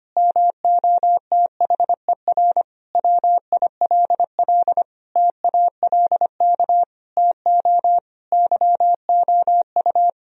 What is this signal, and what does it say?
Morse code, heard as a keyed tone, about 25 wpm, 700 Hz. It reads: MOTHER WILL TALK TO YOU
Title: Sentences from Top 1000 Words - Review Part 1 - 25wpm